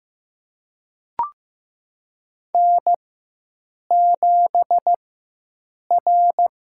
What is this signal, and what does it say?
Morse code, heard as a keyed tone, about 15 wpm, spaced as for 7 wpm, 700 Hz. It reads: N7R
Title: USA Call Signs - 5 Characters - 4x Character Spacing - 15wpm